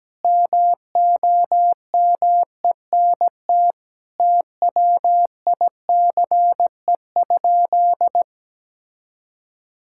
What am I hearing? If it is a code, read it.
Morse code, heard as a keyed tone, about 17 wpm, 700 Hz. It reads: MOMENT TWICE?
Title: Sentences from Top 1600 Words 17wpm